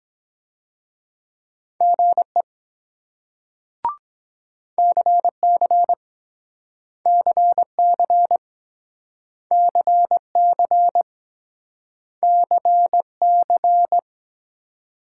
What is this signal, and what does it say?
Morse code, heard as a keyed tone, about 22 wpm, 700 Hz. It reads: GE CC CC CC CC